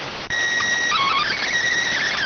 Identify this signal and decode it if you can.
SSTV (slow-scan television), PD 290.